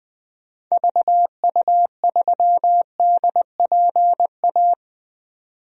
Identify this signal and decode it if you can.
Morse code, heard as a keyed tone, about 21 wpm, 700 Hz. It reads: VU3DPA
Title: DXCC-100 Call Signs - 6 Characters - 20wpm